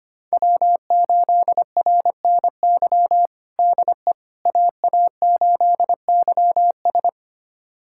Morse code, heard as a keyed tone, about 25 wpm, 700 Hz. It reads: W8RNY DE AA8YS
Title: USA Callsign DE Callsign 25wpm